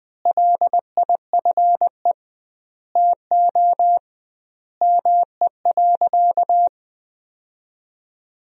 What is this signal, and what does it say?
Morse code, heard as a keyed tone, about 20 wpm, 700 Hz. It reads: LIFE TO ME.
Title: 2x Word-Spacing - Sentences from Top 500 Words - 20wpm